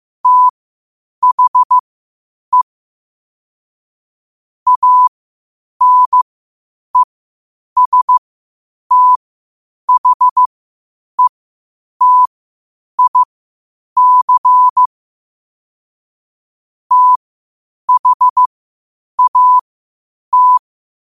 Morse code, heard as a keyed tone, about 15 wpm, spaced as for 8 wpm, 1000 Hz. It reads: THE ANESTHETIC THAT
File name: qotdfiles / 8wpm